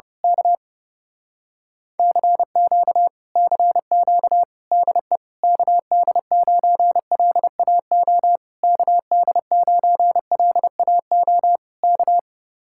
Morse code, heard as a keyed tone, about 28 wpm, 700 Hz. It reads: K CQ CQ DE KD9LAO KD9LAO K
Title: Speed-Racing Common QSO Phrases - 30wpm